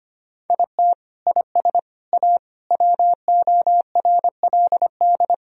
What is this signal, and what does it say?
Morse code, heard as a keyed tone, about 25 wpm, 700 Hz. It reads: IT IS A WORLD
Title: Sentences from Top 1000 Words - Review Part 4 - 25wpm